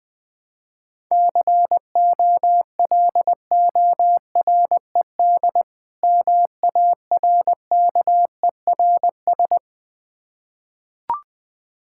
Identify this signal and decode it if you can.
Morse code, heard as a keyed tone, about 20 wpm, 700 Hz. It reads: COLORED MARKERS
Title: Difficult Words with Hints - 20wpm